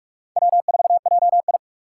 Morse code, heard as a keyed tone, about 41 wpm, 700 Hz. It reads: W4JI
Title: USA Call Signs - 2x Character Spacing - 45wpm